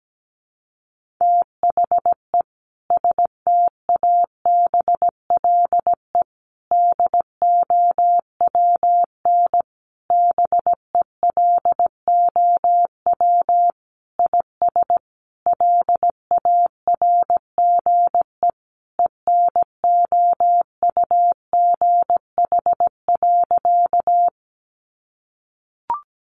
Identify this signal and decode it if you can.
Morse code, heard as a keyed tone, about 17 wpm, 700 Hz. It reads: THE STABLE DOWN BELOW IS LARGE ENOUGH.
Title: Sentences from Top 1500 Words 17wpm